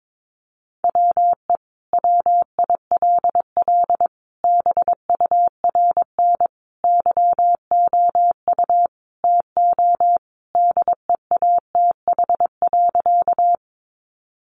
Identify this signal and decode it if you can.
Morse code, heard as a keyed tone, about 22 wpm, 700 Hz. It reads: WE WILL BURN YOU TO DEATH.
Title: Sentences from Top 1300 Words 22wpm